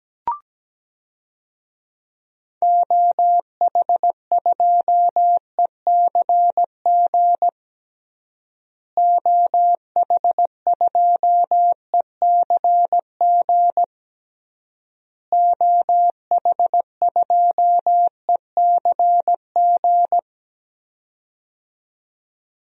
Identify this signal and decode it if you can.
Morse code, heard as a keyed tone, about 17 wpm, 700 Hz. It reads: OH2ECG OH2ECG OH2ECG
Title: DXCC-100 Call Signs - 6 Characters - Repeated 3x - 17wpm